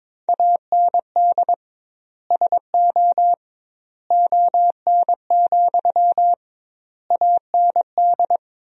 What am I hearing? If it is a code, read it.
Morse code, heard as a keyed tone, about 22 wpm, 700 Hz. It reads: AND SO ON, AND
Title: 2x Word-Spacing - Sentences from Top 100 Words - 22wpm